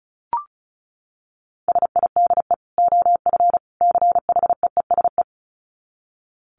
Morse code, heard as a keyed tone, about 34 wpm, 700 Hz. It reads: SIDE OF CHEESE
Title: Dit Challenge 35wpm